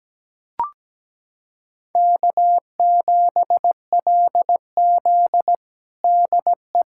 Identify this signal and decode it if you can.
Morse code, heard as a keyed tone, about 17 wpm, 700 Hz. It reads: K7LZ DE
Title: USA Callsign DE Callsign - 4 Characters - 17wpm